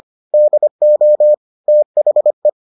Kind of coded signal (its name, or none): Morse code